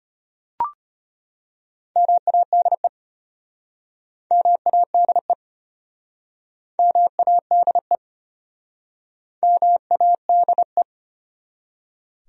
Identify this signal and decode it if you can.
Morse code, heard as a keyed tone, about 33 wpm, 700 Hz. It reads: MADE MADE MADE MADE